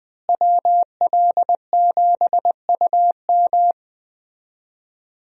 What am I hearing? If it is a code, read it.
Morse code, heard as a keyed tone, about 20 wpm, 700 Hz. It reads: WL7UM
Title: USA Call Signs - 5 Characters - Repeated 3x - 20wpm